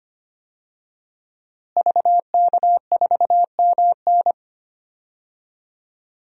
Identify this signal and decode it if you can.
Morse code, heard as a keyed tone, about 25 wpm, 700 Hz. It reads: VK4MN
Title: DXCC-100 Call Signs - 5 Characters - Repeated 2x - 25wpm